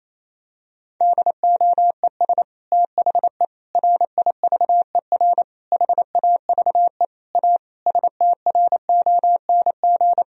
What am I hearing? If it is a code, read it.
Morse code, heard as a keyed tone, about 28 wpm, 700 Hz. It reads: DOES THE RIVER HAVE A STRONG
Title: Sentences from Top 1000 Words - Review Part 4 - 28wpm